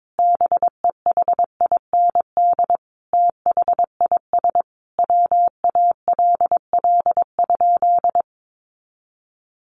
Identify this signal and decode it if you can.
Morse code, heard as a keyed tone, about 22 wpm, 700 Hz. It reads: BEHIND THIS WALL?
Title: Sentences from Top 1100 Words 22wpm